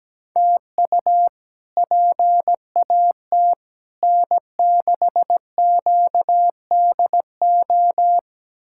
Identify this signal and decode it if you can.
Morse code, heard as a keyed tone, about 17 wpm, 700 Hz. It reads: TU PAT N6QDO
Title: K1USN SST QSO Practice - 17wpm